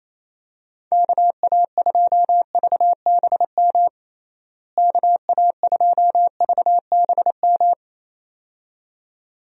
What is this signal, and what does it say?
Morse code, heard as a keyed tone, about 28 wpm, 700 Hz. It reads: KA2VBM KA2VBM
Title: USA Call Signs - 6 Characters - Repeated 2x - 28wpm